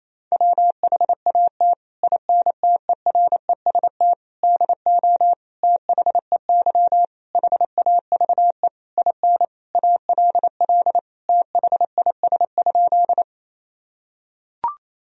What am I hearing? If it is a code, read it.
Morse code, heard as a keyed tone, about 28 wpm, 700 Hz. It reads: WHAT INTEREST DO THEY HAVE IN ALL THIS?
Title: Sentences from Top 1000 Words - Review Part 2 - 28wpm